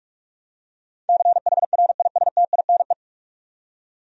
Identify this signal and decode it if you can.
Morse code, heard as a keyed tone, about 41 wpm, 700 Hz. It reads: CHRISTINE
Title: USA Common Names - 45wpm